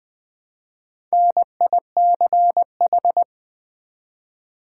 Morse code, heard as a keyed tone, about 20 wpm, 700 Hz. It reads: NICH
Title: Build A Word - 1 Letter At A Time - 5-Letter Words - 20wpm